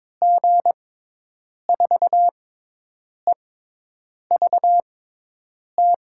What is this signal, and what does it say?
Morse code, heard as a keyed tone, about 21 wpm, spaced as for 8 wpm, 700 Hz. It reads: G4EVT
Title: USA Call Signs - 6 Characters - 6x Character Spacing - 22wpm